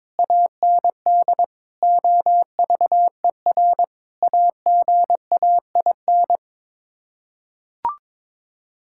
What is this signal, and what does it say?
Morse code, heard as a keyed tone, about 22 wpm, 700 Hz. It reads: AND OVER AGAIN
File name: practice / 22wpm / Sets of 7 Words - 22wpm